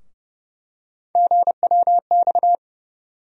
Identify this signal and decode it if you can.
Morse code, heard as a keyed tone, about 30 wpm, 700 Hz. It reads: GWX
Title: USA Call Sign Postfixes - 3 Letters 30wpm